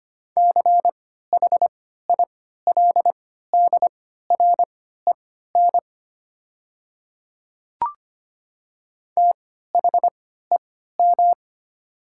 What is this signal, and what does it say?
Morse code, heard as a keyed tone, about 25 wpm, spaced as for 14 wpm, 700 Hz. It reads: CHILDREN THEM
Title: 3x Character Spacing - Top 300 Words - 25wpm